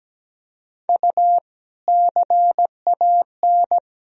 Morse code, heard as a keyed tone, about 17 wpm, 700 Hz. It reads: U CAN
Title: Sentences from Top 400 Words and Abbreviations 17wpm